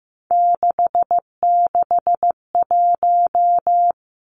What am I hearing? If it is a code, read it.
Morse code, heard as a keyed tone, about 15 wpm, 700 Hz. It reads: 661
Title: Three Challenging Letter-Number Combinations 15wpm